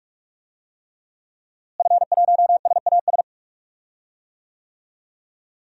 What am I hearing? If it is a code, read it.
Morse code, heard as a keyed tone, about 41 wpm, 700 Hz. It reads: F1SAS
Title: DXCC-100 Call Signs - 5 Characters - Repeated 3x - 45wpm